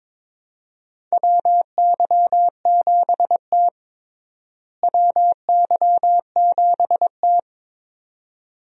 Morse code, heard as a keyed tone, about 22 wpm, 700 Hz. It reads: WY7T WY7T
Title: USA Call Signs - 4 Characters - Repeated 2x - 22wpm